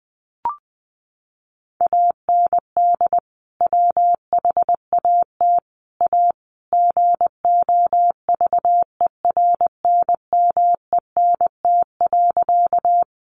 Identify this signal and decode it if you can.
Morse code, heard as a keyed tone, about 20 wpm, 700 Hz. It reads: AND WHAT A GOVERNMENT.
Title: Sentences from Top 600 Words 20wpm